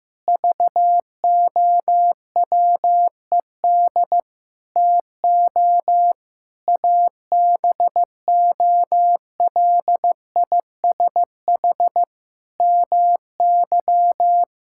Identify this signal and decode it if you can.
Morse code, heard as a keyed tone, about 15 wpm, 700 Hz. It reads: VOWED TO ABOLISH MY
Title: Pangram Challenge v2 - 15wpm